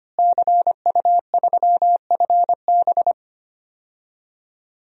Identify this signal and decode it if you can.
Morse code, heard as a keyed tone, about 25 wpm, 700 Hz. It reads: CU3FB
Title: DXCC-100 Call Signs - 5 Characters - Repeated 2x - 25wpm